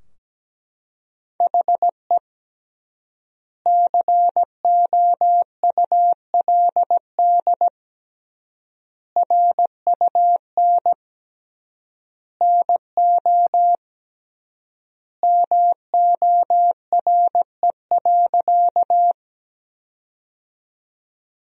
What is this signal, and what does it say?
Morse code, heard as a keyed tone, about 17 wpm, 700 Hz. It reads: HE COULD RUN NO MORE.
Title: 3x Word-Spacing - Sentences from Top 300 Words - 17wpm